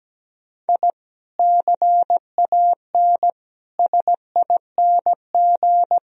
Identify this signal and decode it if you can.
Morse code, heard as a keyed tone, about 17 wpm, 700 Hz. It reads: I CAN SING
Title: Sentences from Top 1000 Words - Review Part 2 - 17wpm